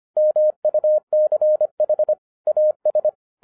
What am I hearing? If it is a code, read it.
Morse code, heard as a keyed tone, about 25 wpm, 610 Hz. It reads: MUCH AS